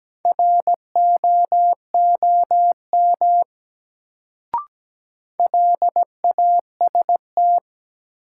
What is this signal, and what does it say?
Morse code, heard as a keyed tone, about 17 wpm, 700 Hz. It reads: ROOM LAST
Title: Sentences from Top 300 Words - New Word Drill - 17wpm